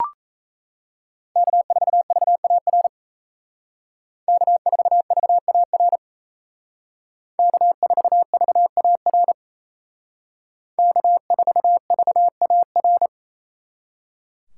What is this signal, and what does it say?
Morse code, heard as a keyed tone, about 37 wpm, 700 Hz. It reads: K4VAR K4VAR K4VAR K4VAR